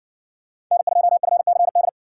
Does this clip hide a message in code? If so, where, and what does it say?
Yes: Morse code from 0.7 to 1.9 s, heard as a keyed tone, about 60 wpm, 700 Hz. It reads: N2FCD